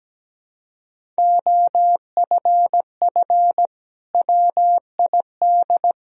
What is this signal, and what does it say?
Morse code, heard as a keyed tone, about 17 wpm, 700 Hz. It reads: OFF WID
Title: Sentences from Top 300 Words and Abbreviations 17wpm